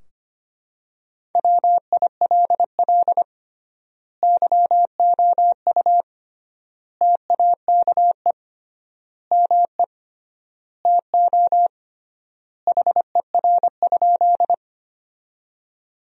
Morse code, heard as a keyed tone, about 25 wpm, 700 Hz. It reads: WILL YOU TAKE ME TO HER?
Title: 3x Word-Spacing - Sentences from Top 200 Words - 25wpm